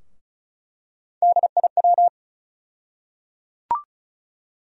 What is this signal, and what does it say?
Morse code, heard as a keyed tone, about 36 wpm, 700 Hz. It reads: DIW
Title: USA Call Sign Postfixes - 3 Letters 35wpm